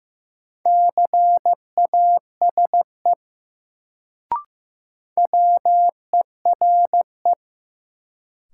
Morse code, heard as a keyed tone, about 15 wpm, 700 Hz. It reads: CASE WERE
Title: Sets of 2 Words - New Word Drill - 15wpm